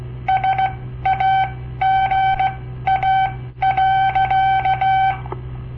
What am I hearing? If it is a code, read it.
Morse code, heard as a keyed tone, about 17 wpm, spaced as for 13 wpm, 770 Hz. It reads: SAGA.
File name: papertapes / local / Code Practice-01